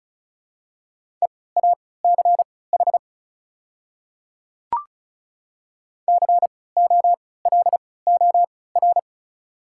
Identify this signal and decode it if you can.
Morse code, heard as a keyed tone, about 36 wpm, spaced as for 20 wpm, 700 Hz. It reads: EACH COLOR
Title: 3x Character Spacing - Top 300 Words - 35wpm